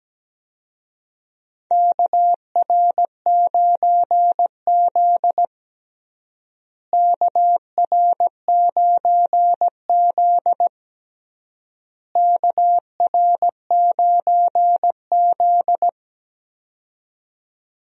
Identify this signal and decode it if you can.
Morse code, heard as a keyed tone, about 17 wpm, 700 Hz. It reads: KR9Z KR9Z KR9Z
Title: USA Call Signs - 4 Characters - Repeated 3x - 17wpm